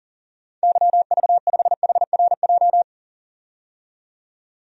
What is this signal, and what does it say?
Morse code, heard as a keyed tone, about 40 wpm, 700 Hz. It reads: YV5HRJ
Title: DXCC-100 Call Signs - 6 Characters - 40wpm